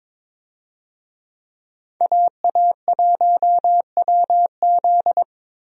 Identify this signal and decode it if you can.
Morse code, heard as a keyed tone, about 22 wpm, 700 Hz. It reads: AA1WZ
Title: USA Call Signs - 5 Characters - Repeated 2x - 22wpm